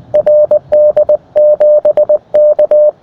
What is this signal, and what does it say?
Morse code, heard as a keyed tone, about 19 wpm, 600 Hz. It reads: RD7K
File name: attachments / rd7k 20m